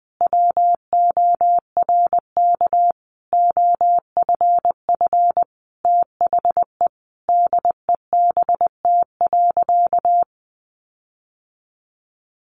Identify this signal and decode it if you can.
Morse code, heard as a keyed tone, about 20 wpm, 700 Hz. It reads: WORK OFF THE DEBT.
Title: Sentences from Top 1300 Words 20wpm